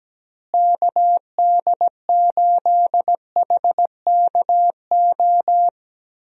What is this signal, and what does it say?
Morse code, heard as a keyed tone, about 17 wpm, 700 Hz. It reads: KD8HKO
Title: USA Call Signs - 6 Characters - Repeated 3x - 17wpm